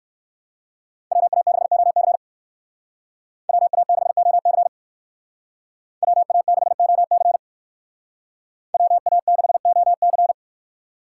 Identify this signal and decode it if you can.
Morse code, heard as a keyed tone, about 58 wpm, 700 Hz. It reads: WA6OC WA6OC WA6OC WA6OC